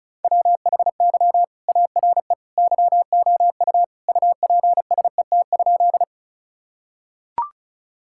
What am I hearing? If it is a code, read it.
Morse code, heard as a keyed tone, about 36 wpm, 700 Hz. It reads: WHY ARE YOU UPSET?
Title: Sentences from Top 1000 Words - Review Part 5 - 35wpm